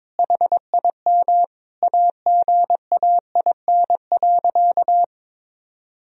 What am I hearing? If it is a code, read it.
Morse code, heard as a keyed tone, about 22 wpm, 700 Hz. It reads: HIM AGAIN.
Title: Sentences from Top 1000 Words - Review Part 1 - 22wpm